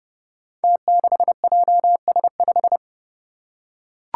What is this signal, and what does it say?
Morse code, heard as a keyed tone, about 30 wpm, 700 Hz. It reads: T6JS5
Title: Five Challenging Letter-Number Combinations 30wpm